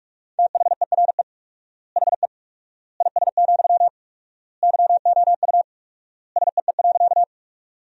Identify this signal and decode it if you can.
Morse code, heard as a keyed tone, about 41 wpm, 700 Hz. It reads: THERE HE IS, YOU SEE.
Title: 4x Word-Spacing - Sentences from Top 100 Words - 45wpm